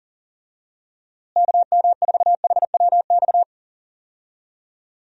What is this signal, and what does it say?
Morse code, heard as a keyed tone, about 40 wpm, 700 Hz. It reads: KM4HWX
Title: USA Call Signs - 6 Characters - Repeated 2x - 40wpm